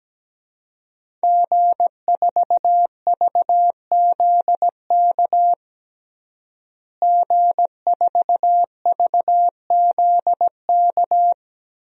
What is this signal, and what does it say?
Morse code, heard as a keyed tone, about 17 wpm, 700 Hz. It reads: G4VZK G4VZK
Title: DXCC-100 Call Signs - 5 Characters - Repeated 2x - 17wpm